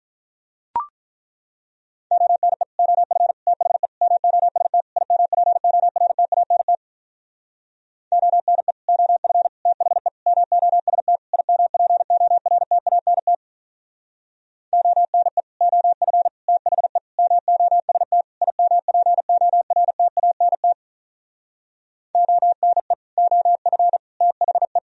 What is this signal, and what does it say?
Morse code, heard as a keyed tone, about 43 wpm, 700 Hz. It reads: ONE OF THE MOST IMPORTANT ONE OF THE MOST IMPORTANT ONE OF THE MOST IMPORTANT ONE OF THE
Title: Speed-Racing - Sets of 5 Words 35wpm